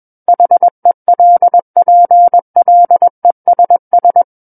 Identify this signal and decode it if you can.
Morse code, heard as a keyed tone, about 21 wpm, 700 Hz. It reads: HELPLESS
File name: Book One / 12-Chapter 12 - 21 WPM